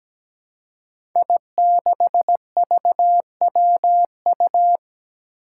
Morse code, heard as a keyed tone, about 17 wpm, 700 Hz. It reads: I6VWU
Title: Five Challenging Letter-Number Combinations 17wpm